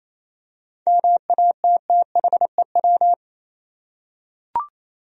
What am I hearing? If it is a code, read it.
Morse code, heard as a keyed tone, about 27 wpm, 700 Hz. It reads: MATTHEW
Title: USA Common Names - 28wpm